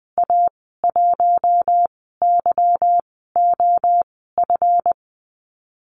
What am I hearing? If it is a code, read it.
Morse code, heard as a keyed tone, about 20 wpm, spaced as for 15 wpm, 700 Hz. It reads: A1YOF
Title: 154 Lesson 25g - Letter p - Call Signs 20wpm